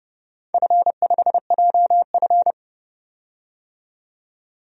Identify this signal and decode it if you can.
Morse code, heard as a keyed tone, about 30 wpm, 700 Hz. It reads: F5JF
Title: DXCC-100 Call Signs - 4 Characters - 30wpm